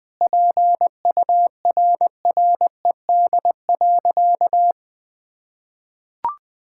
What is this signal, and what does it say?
Morse code, heard as a keyed tone, about 20 wpm, 700 Hz. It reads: PURRED.
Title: Sentences with 2-Word Phrases - 20wpm